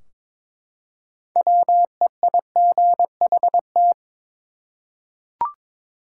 Morse code, heard as a keyed tone, about 22 wpm, 700 Hz. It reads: WEIGHT
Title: Six Letter Words - 22wpm